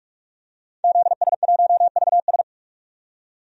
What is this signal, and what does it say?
Morse code, heard as a keyed tone, about 46 wpm, 700 Hz. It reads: ZS1VS
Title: DXCC-100 Call Signs - 5 Characters - 45wpm